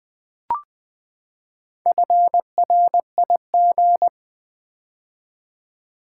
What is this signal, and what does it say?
Morse code, heard as a keyed tone, about 20 wpm, 700 Hz. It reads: FRIG
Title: Build A Word - 4 Letters At A Time - 11-Letter Words - 20wpm